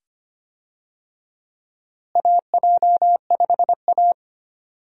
Morse code, heard as a keyed tone, about 25 wpm, 700 Hz. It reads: AJ5A
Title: USA Call Signs - 4 Characters - Repeated 2x - 25wpm